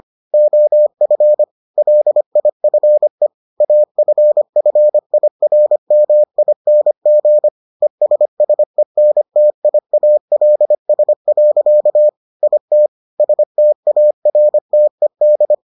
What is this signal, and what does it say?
Morse code, heard as a keyed tone, about 25 wpm, 600 Hz. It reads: OF LIFE AFFIRMING ESSENTIALS. IT STARTED